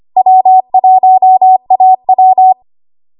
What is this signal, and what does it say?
Morse code, heard as a keyed tone, about 25 wpm, 750 Hz. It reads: W1AW
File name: Morse / 25WPM